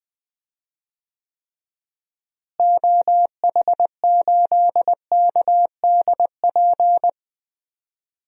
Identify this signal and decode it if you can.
Morse code, heard as a keyed tone, about 20 wpm, 700 Hz. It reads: OH8KDP
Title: DXCC-100 Call Signs - 6 Characters - Repeated 2x - 20wpm